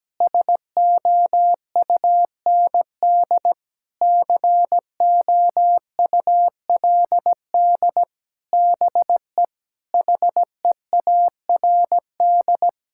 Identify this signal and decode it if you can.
Morse code, heard as a keyed tone, about 17 wpm, 700 Hz. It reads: SOUND COULD BE HEARD
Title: Sentences from Top 1000 Words - Review Part 5 - 17wpm